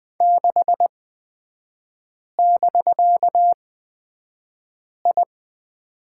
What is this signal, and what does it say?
Morse code, heard as a keyed tone, about 20 wpm, spaced as for 5 wpm, 700 Hz. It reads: 6<BK>I
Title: 250 Lesson 40e - BK - Three Characters 20wpm